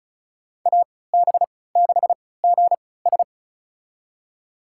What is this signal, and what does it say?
Morse code, heard as a keyed tone, about 36 wpm, spaced as for 20 wpm, 690 Hz. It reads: AB6GS
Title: USA Call Signs - 5 Characters - 3x Character Spacing - 35wpm